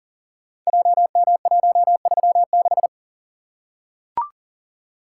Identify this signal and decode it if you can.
Morse code, heard as a keyed tone, about 39 wpm, 700 Hz. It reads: JM136
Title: Code Groups - 5 Letter:Numbers - 40wpm